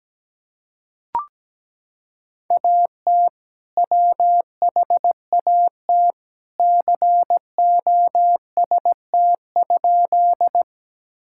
Morse code, heard as a keyed tone, about 17 wpm, 700 Hz. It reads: AT WHAT COST?
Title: Sentences from Top 1000 Words - Review Part 5 - 17wpm